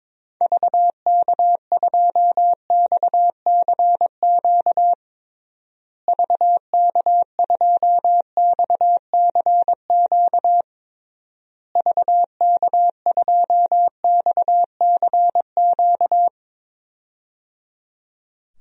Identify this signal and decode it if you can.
Morse code, heard as a keyed tone, about 22 wpm, 700 Hz. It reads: VK2XCQ VK2XCQ VK2XCQ
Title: DXCC-100 Call Signs - 6 Characters - Repeated 3x - 22wpm